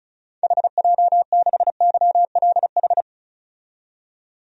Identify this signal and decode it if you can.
Morse code, heard as a keyed tone, about 34 wpm, 700 Hz. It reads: HJ6YLH
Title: DXCC-100 Call Signs - 6 Characters - 35wpm